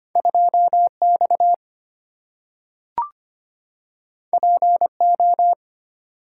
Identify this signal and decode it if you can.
Morse code, heard as a keyed tone, about 25 wpm, 700 Hz. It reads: 2 X P O
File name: Two Letter-Number Combinations - 25wpm